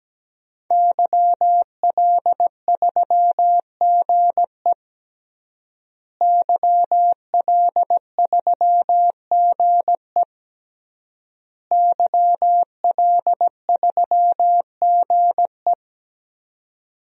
Morse code, heard as a keyed tone, about 17 wpm, 700 Hz. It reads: YL3GE YL3GE YL3GE
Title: DXCC-100 Call Signs - 5 Characters - Repeated 3x - 17wpm